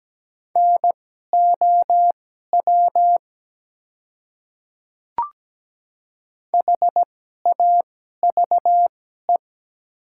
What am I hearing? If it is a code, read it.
Morse code, heard as a keyed tone, about 17 wpm, spaced as for 12 wpm, 700 Hz. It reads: NOW HAVE
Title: 2x Character Spacing - Top 100 Words - 17wpm